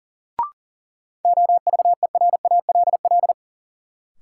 Morse code, heard as a keyed tone, about 37 wpm, 700 Hz. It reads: OVERALL
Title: Sentences from Top 1000 Words - Review Part 7 - New Word Drill - 40wpm